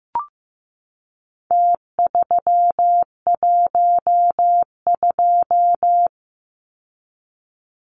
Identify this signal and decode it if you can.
Morse code, heard as a keyed tone, about 15 wpm, 700 Hz. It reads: T312
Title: Numbers - Contest Style - 4-Digit 15wpm